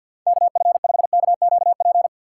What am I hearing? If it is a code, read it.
Morse code, heard as a keyed tone, about 52 wpm, 700 Hz. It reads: KF5KQP
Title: USA Call Signs - 2x Character Spacing - 50wpm